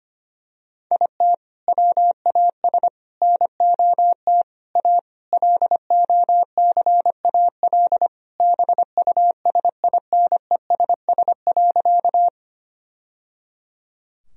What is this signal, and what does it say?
Morse code, heard as a keyed tone, about 25 wpm, 700 Hz. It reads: IT WAS NOT A LOCAL BUSINESS.
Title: Sentences from Top 1000 Words - Review Part 3 - 25wpm